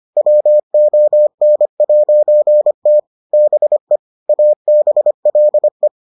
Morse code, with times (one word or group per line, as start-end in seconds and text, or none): 0.2-3.0 WON'T
3.3-4.0 BE
4.3-5.9 ABLE